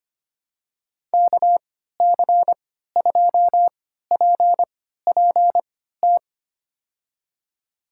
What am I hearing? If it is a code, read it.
Morse code, heard as a keyed tone, about 25 wpm, spaced as for 14 wpm, 700 Hz. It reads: KC2PPT
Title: USA Call Signs - 6 Characters - 3x Character Spacing - 25wpm